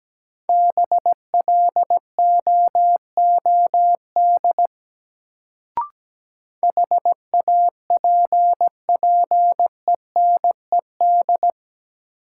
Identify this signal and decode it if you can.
Morse code, heard as a keyed tone, about 17 wpm, 700 Hz. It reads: BLOOD HAPPENED
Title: Sentences from Top 1000 Words - Review Part 2 - New Word Drill - 17wpm